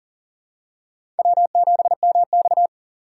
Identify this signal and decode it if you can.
Morse code, heard as a keyed tone, about 40 wpm, 700 Hz. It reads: W7MX
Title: USA Call Signs - 4 Characters - 4x Character Spacing - 40wpm